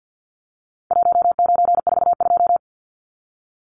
Morse code, heard as a keyed tone, about 50 wpm, 700 Hz. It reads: 1942
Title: Numbers - Contest Style - 4-Digit 50wpm